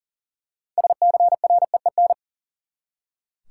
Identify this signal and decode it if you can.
Morse code, heard as a keyed tone, about 37 wpm, 700 Hz. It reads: SCREEN
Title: Six Letter Words - 40wpm